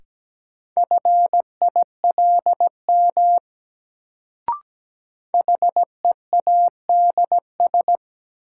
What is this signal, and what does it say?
Morse code, heard as a keyed tone, about 17 wpm, 700 Hz. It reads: FILM HEADS
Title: Sentences from Top 1000 Words - Review Part 3 - New Word Drill - 17wpm